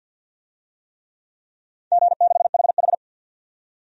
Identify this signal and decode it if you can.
Morse code, heard as a keyed tone, about 46 wpm, 700 Hz. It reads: G6HH
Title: DXCC-100 Call Signs - 4 Characters - 50wpm